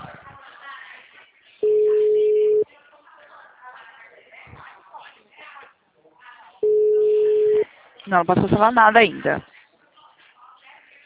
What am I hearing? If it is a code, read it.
Ringback tone (Europe, 425 Hz), with speech.
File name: out- -2006- - - .526